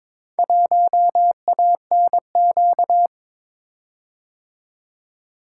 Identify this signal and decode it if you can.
Morse code, heard as a keyed tone, about 22 wpm, 700 Hz. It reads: 1ANQ
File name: practice / 22wpm / Four Letter-Number Combinations - 22wpm